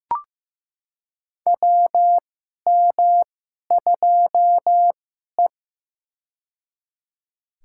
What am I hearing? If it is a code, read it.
Morse code, heard as a keyed tone, about 15 wpm, spaced as for 11 wpm, 700 Hz. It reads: WM2E
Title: USA Call Signs - 2x Character Spacing - 15wpm